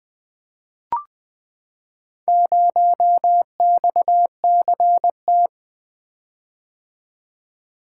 Morse code, heard as a keyed tone, about 20 wpm, 700 Hz. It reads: 0XCT
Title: Four Letter-Number Combinations - 20wpm